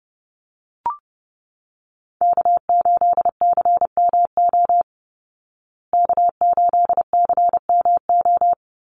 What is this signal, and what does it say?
Morse code, heard as a keyed tone, about 30 wpm, 700 Hz. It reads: K8CMO K8CMO
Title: Call Signs Repeated 3x - 30wpm